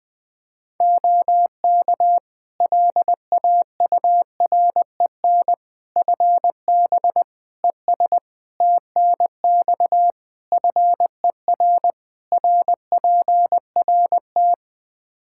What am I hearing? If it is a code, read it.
Morse code, heard as a keyed tone, about 20 wpm, 700 Hz. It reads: OK LAUREN FB ES TNX FER RPRT
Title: Practice QSO - Exchange 2 - Info - 20wpm